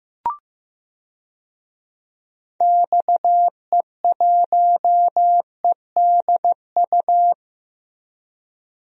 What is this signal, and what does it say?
Morse code, heard as a keyed tone, about 16 wpm, 700 Hz. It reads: XE1EDU
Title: DXCC-100 Call Signs - 6 Characters - Repeated 3x - 15wpm